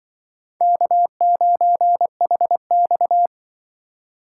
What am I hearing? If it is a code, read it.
Morse code, heard as a keyed tone, about 24 wpm, 700 Hz. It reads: K9HX
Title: Speed Racing - USA 1x2 Call Signs 20wpm